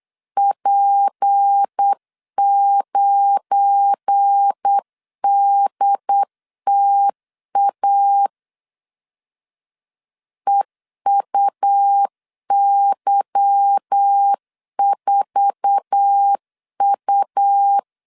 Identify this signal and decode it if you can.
Morse code, heard as a keyed tone, about 8 wpm, 800 Hz. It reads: P9DTA EUY4U
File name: LEZIONI / 8 YJ45 / 8 YJ4 8wpm 3246